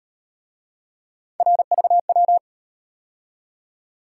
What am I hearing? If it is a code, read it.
Morse code, heard as a keyed tone, about 38 wpm, 700 Hz. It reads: RVW